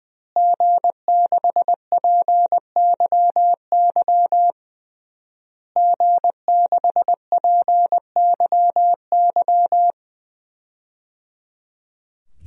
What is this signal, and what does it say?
Morse code, heard as a keyed tone, about 20 wpm, 700 Hz. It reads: G6PYY G6PYY
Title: DXCC-100 Call Signs - 5 Characters - Repeated 3x - 20wpm